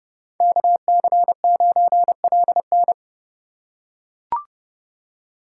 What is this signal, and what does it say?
Morse code, heard as a keyed tone, about 30 wpm, 690 Hz. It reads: KC9LN
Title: USA Call Signs - 5 Characters - 2x Character Spacing - 30wpm